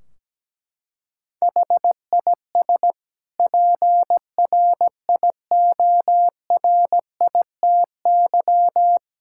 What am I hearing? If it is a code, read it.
Morse code, heard as a keyed tone, about 17 wpm, 700 Hz. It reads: HIS PRIORITY
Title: Sentences from Top 1600 Words 17wpm